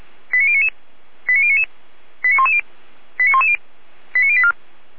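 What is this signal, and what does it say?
Five-tone selective call (ZVEI), heard as quick runs of five tones, tones of about 70 ms. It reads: 89099 89099 89110 89110 89084
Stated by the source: ZVEI five-tone selective call received by SDR.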